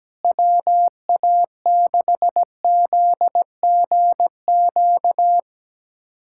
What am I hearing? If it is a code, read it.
Morse code, heard as a keyed tone, about 17 wpm, 700 Hz. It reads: WA6ZGQ
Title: USA Callsign DE Callsign - 6 Characters - 17wpm